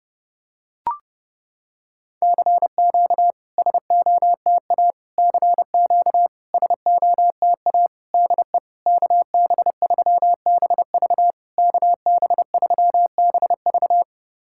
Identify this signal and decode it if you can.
Morse code, heard as a keyed tone, about 30 wpm, 700 Hz. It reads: CQ SOTA CQ SOTA DE KB3BV KB3BV
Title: SOTA QSO Practice 30wpm